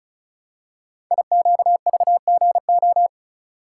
Morse code, heard as a keyed tone, about 36 wpm, 700 Hz. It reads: IQVGO
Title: Code Groups - 5 Letters - 35wpm